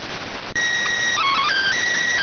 SSTV (slow-scan television), PD 290.